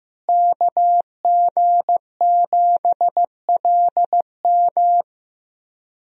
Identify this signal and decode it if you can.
Morse code, heard as a keyed tone, about 15 wpm, 700 Hz. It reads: KG7LM
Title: USA Call Signs - 5 Characters - Repeated 2x - 15wpm